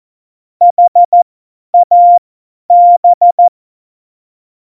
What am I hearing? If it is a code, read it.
Morse code, heard as a keyed tone, about 14 wpm, spaced as for 10 wpm, 700 Hz. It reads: HAB
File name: Morsekurs / Witze / Witz 18. September 2018, 50 F70